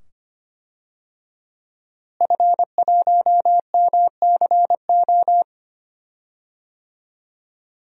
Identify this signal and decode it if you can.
Morse code, heard as a keyed tone, about 25 wpm, 700 Hz. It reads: F1MCO
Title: DXCC-100 Call Signs - 5 Characters - Repeated 2x - 25wpm